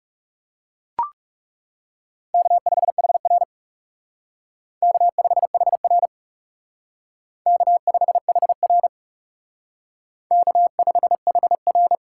Morse code, heard as a keyed tone, about 37 wpm, 700 Hz. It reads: K5HR K5HR K5HR K5HR
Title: Speed Racing - USA 1x2 Call Signs 30wpm